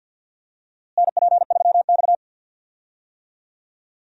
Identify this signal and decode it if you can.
Morse code, heard as a keyed tone, about 46 wpm, 700 Hz. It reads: NP3X
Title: USA Call Signs - 4 Characters - Repeated 2x - 50wpm